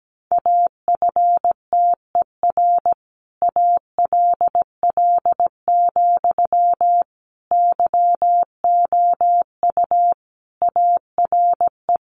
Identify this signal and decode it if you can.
Morse code, heard as a keyed tone, about 17 wpm, 700 Hz. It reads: AFTER ALL, YOU ARE
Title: Sentences from Top 600 Words 17wpm